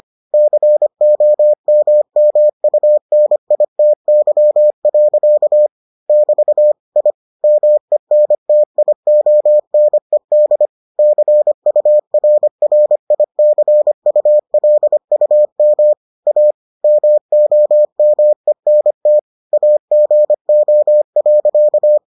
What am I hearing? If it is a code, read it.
Morse code, heard as a keyed tone, about 25 wpm, 600 Hz. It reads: COMMUNITY. <BT> I MENTIONED CURRICULUM A MOMENT AGO.